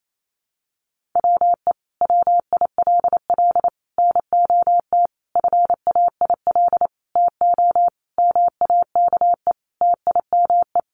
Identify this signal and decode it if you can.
Morse code, heard as a keyed tone, about 28 wpm, 700 Hz. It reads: WE WILL NOT FAIL TO MAKE TIME
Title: Sentences from Top 1100 Words 28wpm